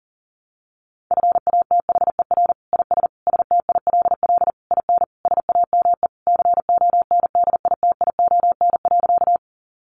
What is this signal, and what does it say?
Morse code, heard as a keyed tone, about 37 wpm, 700 Hz. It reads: FATHER IS STILL IN SAME CONDITION.
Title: Sentences from Top 1000 Words 40wpm